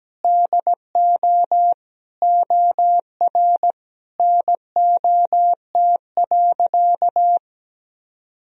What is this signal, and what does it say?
Morse code, heard as a keyed tone, about 17 wpm, 700 Hz. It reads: DO OR NOT.
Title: Sentences from Top 1000 Words - Review Part 3 - 17wpm